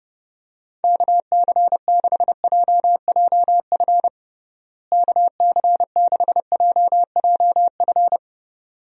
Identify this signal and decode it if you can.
Morse code, heard as a keyed tone, about 30 wpm, 700 Hz. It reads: KC6JJF KC6JJF
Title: USA Call Signs - 6 Characters - Repeated 2x - 30wpm